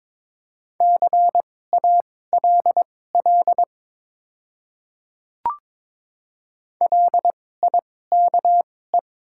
Morse code, heard as a keyed tone, about 22 wpm, spaced as for 16 wpm, 700 Hz. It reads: CALL LIKE
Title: 2x Character Spacing - Top 200 Words - 22wpm